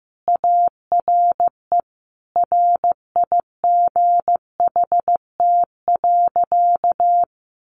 Morse code, heard as a keyed tone, about 15 wpm, 700 Hz. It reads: ARE RIGHT.
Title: Sentences from Top 500 Words 15wpm